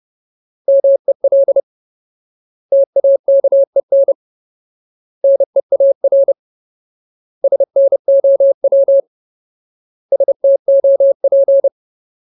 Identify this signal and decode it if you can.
Morse code, heard as a keyed tone, about 30 wpm, 550 Hz. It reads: MEL TAKEN NEAR SNOW STOP
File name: Ultimate Word List / 30WPM Ultimate